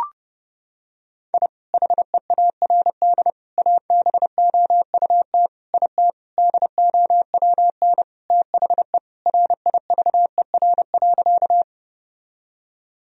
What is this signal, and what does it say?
Morse code, heard as a keyed tone, about 30 wpm, 700 Hz. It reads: I HEARD ABOUT IT DOWN THE RIVER.
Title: Sentences from Top 1000 Words - Review Part 1 - 30wpm